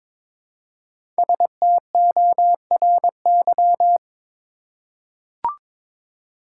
Morse code, heard as a keyed tone, about 22 wpm, 700 Hz. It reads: STORY
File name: 5 Letter Words - First 3 Letters Given - 22wpm